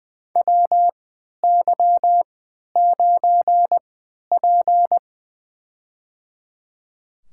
Morse code, heard as a keyed tone, about 20 wpm, spaced as for 11 wpm, 700 Hz. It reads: WY9P
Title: USA Call Signs - 4 Characters - 3x Character Spacing - 20wpm